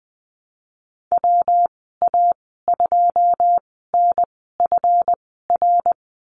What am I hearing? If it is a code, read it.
Morse code, heard as a keyed tone, about 20 wpm, spaced as for 15 wpm, 700 Hz. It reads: WA2NFR
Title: 173 Lesson 28g - Number 9 - Call Signs 20wpm